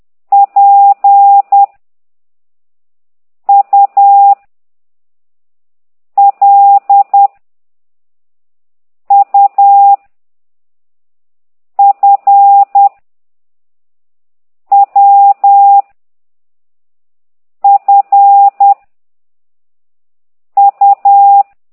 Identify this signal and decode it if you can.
Morse code, heard as a keyed tone, about 10 wpm, 800 Hz. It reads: P U L U F W F U
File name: ct2idw / exerc8